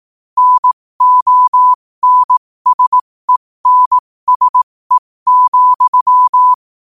Morse code, heard as a keyed tone, about 18 wpm, spaced as for 15 wpm, 1000 Hz. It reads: NONSENSE,
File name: qotdfiles / 15wpm